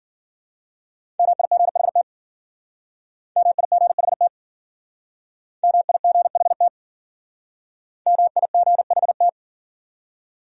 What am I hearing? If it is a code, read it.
Morse code, heard as a keyed tone, about 49 wpm, 700 Hz. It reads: MIGHT MIGHT MIGHT MIGHT
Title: Speed-Racing Top 500 Words 40wpm